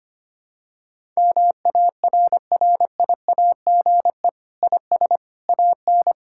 Morse code, heard as a keyed tone, about 25 wpm, 700 Hz. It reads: MARRIAGE IS AN
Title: Sentences from Top 1000 Words - Review Part 5 - 25wpm